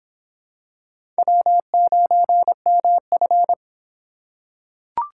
Morse code, heard as a keyed tone, about 25 wpm, 700 Hz. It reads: W9MF